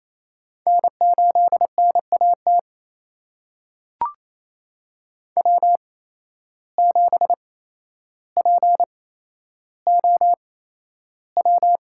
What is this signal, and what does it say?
Morse code, heard as a keyed tone, about 28 wpm, spaced as for 12 wpm, 700 Hz. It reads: N8NAT W7POW
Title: USA Call Signs - 8x Character Spacing - 28wpm